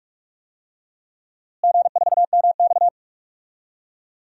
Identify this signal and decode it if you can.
Morse code, heard as a keyed tone, about 46 wpm, 700 Hz. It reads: G4MX
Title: DXCC-100 Call Signs - 4 Characters - Repeated 3x - 45wpm